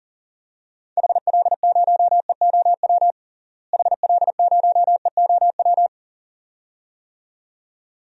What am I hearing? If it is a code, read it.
Morse code, heard as a keyed tone, about 40 wpm, 700 Hz. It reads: HL0EOW HL0EOW
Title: DXCC-100 Call Signs - 6 Characters - Repeated 2x - 40wpm